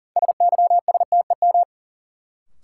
Morse code, heard as a keyed tone, about 40 wpm, 700 Hz. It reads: SYSTEM
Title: Sentences from Top 1000 Words - Review Part 3 - New Word Drill - 40wpm